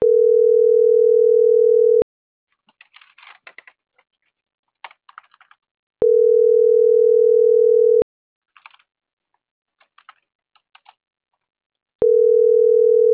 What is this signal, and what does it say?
Ringback tone (North America), 440 + 480 Hz.